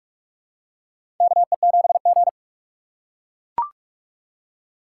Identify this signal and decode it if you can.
Morse code, heard as a keyed tone, about 44 wpm, 700 Hz. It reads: KE7G
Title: USA Call Signs - 4 Characters - 3x Character Spacing - 45wpm